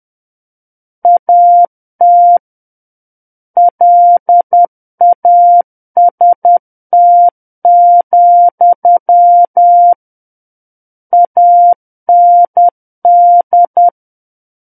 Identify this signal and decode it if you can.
Morse code, heard as a keyed tone, about 10 wpm, 700 Hz. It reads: AT LAST, AND
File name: Book One / 01-Chapter 1 - 10 WPM